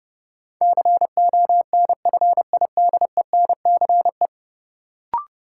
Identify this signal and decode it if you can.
Morse code, heard as a keyed tone, about 30 wpm, 700 Hz. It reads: CONFIDENCE
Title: Sentences from Top 1500 Words - New Word Drill Alt - 30wpm